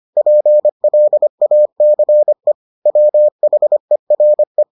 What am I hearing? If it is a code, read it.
Morse code, heard as a keyed tone, about 25 wpm, 600 Hz. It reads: PLACE WHERE